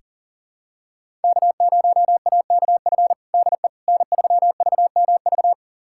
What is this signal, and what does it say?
Morse code, heard as a keyed tone, about 40 wpm, 700 Hz. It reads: K0AKF DE N3VMV
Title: USA Callsign DE Callsign - 5 Characters - 40wpm